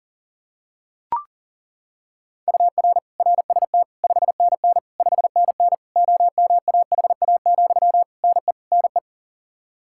Morse code, heard as a keyed tone, about 40 wpm, 700 Hz. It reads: UR RST 5NN 5NN OMAHA, NE NE
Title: Practice QSO - Exchange 1 - Beginning - 40wpm